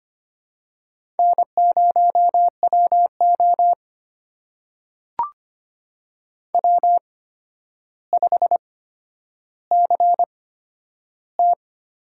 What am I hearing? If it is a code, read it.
Morse code, heard as a keyed tone, about 25 wpm, spaced as for 11 wpm, 700 Hz. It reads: N0WO W5CT
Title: USA Call Signs - 4 Characters - 8x Character Spacing - 25wpm